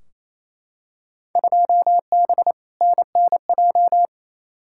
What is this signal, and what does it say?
Morse code, heard as a keyed tone, about 28 wpm, 700 Hz. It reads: 2B NNJ
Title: Field Day Exchanges Encore - 28wpm